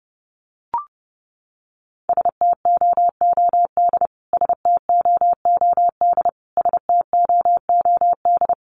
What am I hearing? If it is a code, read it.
Morse code, heard as a keyed tone, about 28 wpm, 700 Hz. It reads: STOOD STOOD STOOD
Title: Top 500 Words Repeated 30wpm